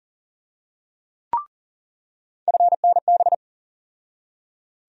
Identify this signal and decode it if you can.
Morse code, heard as a keyed tone, about 40 wpm, 700 Hz. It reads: FNB
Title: USA Call Sign Postfixes - 3 Letters 40wpm